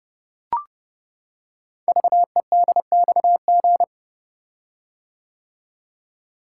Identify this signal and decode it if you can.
Morse code, heard as a keyed tone, about 30 wpm, 700 Hz. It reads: VEDXG
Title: Code Groups - 5 Letters - 30wpm